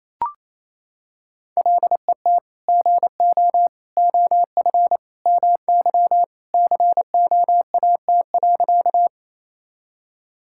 Morse code, heard as a keyed tone, about 28 wpm, 700 Hz. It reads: LET GO OF MY COAT.
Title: Sentences from Top 1600 Words 28wpm